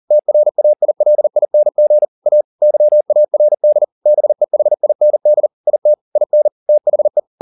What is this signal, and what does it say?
Morse code, heard as a keyed tone, about 40 wpm, 600 Hz. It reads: TRAILING A YARD BEHIND IT IN THE